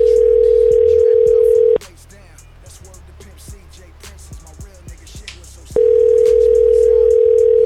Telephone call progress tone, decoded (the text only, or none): ringback tone